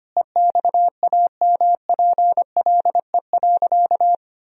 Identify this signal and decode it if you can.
Morse code, heard as a keyed tone, about 25 wpm, 700 Hz. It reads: EXAMPLE.
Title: Sentences from Top 1000 Words - Review Part 1 - 25wpm